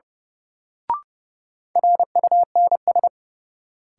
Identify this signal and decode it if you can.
Morse code, heard as a keyed tone, about 30 wpm, 700 Hz. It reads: RUNS
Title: Sentences from Top 600 Words - New Word Drill - 30wpm